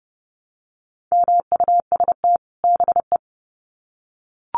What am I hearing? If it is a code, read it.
Morse code, heard as a keyed tone, about 30 wpm, 700 Hz. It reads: MUST BE
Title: Sets of 2 Words 30wpm